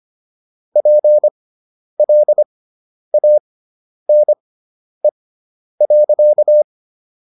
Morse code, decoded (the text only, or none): PLANE.